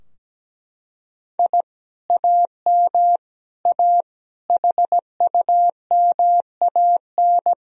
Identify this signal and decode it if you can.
Morse code, heard as a keyed tone, about 17 wpm, 700 Hz. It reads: I AM A HUMAN
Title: Sentences from Top 1000 Words - Review Part 3 - 17wpm